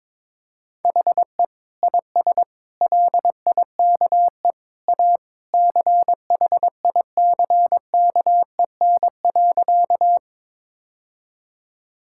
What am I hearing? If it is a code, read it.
Morse code, heard as a keyed tone, about 21 wpm, 700 Hz. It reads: HE IS LIKE A CHICKEN.
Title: Sentences from Top 1000 Words - Review Part 5 - 22wpm